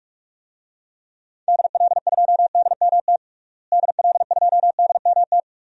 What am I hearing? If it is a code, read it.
Morse code, heard as a keyed tone, about 42 wpm, 700 Hz. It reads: DL2DMT DL2DMT
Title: DXCC-100 Call Signs - 6 Characters - Repeated 2x - 45wpm